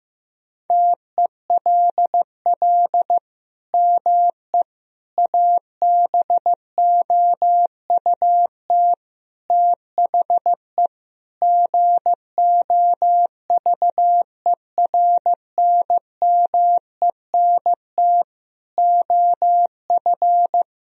Morse code, heard as a keyed tone, about 15 wpm, 700 Hz. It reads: TELL ME ABOUT THE GOVERNMENT OF
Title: Sentences from Top 1000 Words - Review Part 3 - 15wpm